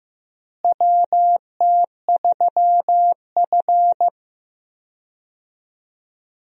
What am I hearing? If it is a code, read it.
Morse code, heard as a keyed tone, about 15 wpm, 700 Hz. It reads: WT3F
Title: USA Call Signs - 4 Characters - Repeated 3x - 15wpm